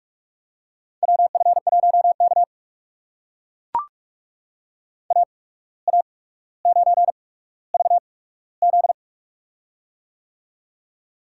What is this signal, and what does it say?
Morse code, heard as a keyed tone, about 44 wpm, spaced as for 18 wpm, 690 Hz. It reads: WF1K AA9VZ
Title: USA Call Signs - 8x Character Spacing - 45wpm